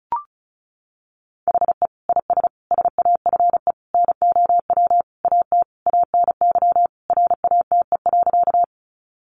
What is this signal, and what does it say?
Morse code, heard as a keyed tone, about 34 wpm, 700 Hz. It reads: HE IS SAFE NOW AT ANY RATE.
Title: Sentences from Top 900 Words 35wpm